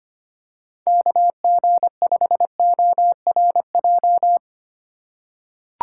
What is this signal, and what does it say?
Morse code, heard as a keyed tone, about 25 wpm, 700 Hz. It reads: KG5ORJ
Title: USA Call Signs - 3x Character Spacing - 25wpm